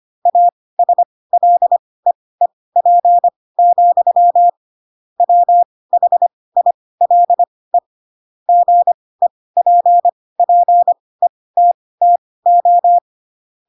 Morse code, heard as a keyed tone, about 25 wpm, spaced as for 18 wpm, 700 Hz. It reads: ASLEEP, WHILE GEPPETTO